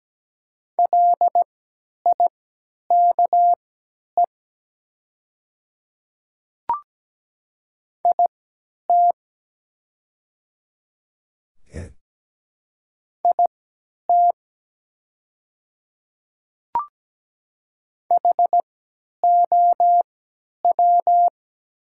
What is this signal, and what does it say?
Morse code, heard as a keyed tone, about 17 wpm, spaced as for 10 wpm, 700 Hz. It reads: LIKE IT IT HOW
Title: 3x Character Spacing - Top 100 Words - 17wpm